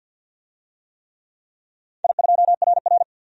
Morse code, heard as a keyed tone, about 51 wpm, 700 Hz. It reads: I2RR
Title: DXCC-100 Call Signs - 4 Characters - 50wpm